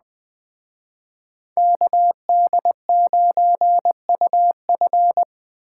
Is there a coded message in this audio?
Yes: Morse code from 1.6 to 5.2 s, heard as a keyed tone, about 20 wpm, 700 Hz. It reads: KD9UF